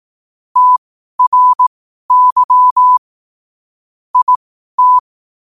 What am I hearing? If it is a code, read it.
Morse code, heard as a keyed tone, about 18 wpm, spaced as for 12 wpm, 1000 Hz. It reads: TRY IT
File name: qotdfiles / 12wpm